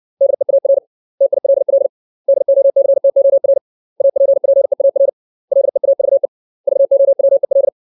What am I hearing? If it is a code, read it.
Morse code, heard as a keyed tone, about 60 wpm, 550 Hz. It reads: DEAR TILL DOCTOR APPEAR LEAVE VOWEL
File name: 632 words / 632 Words 60WPM